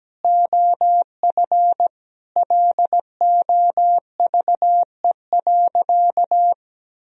Morse code, heard as a keyed tone, about 17 wpm, 700 Hz. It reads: OF LOVE.
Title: Sentences from Top 1000 Words - Review Part 3 - 17wpm